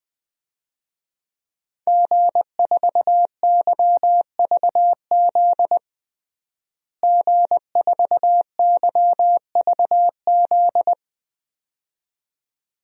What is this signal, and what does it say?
Morse code, heard as a keyed tone, about 20 wpm, 700 Hz. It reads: G4YVZ G4YVZ
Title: DXCC-100 Call Signs - 5 Characters - Repeated 2x - 20wpm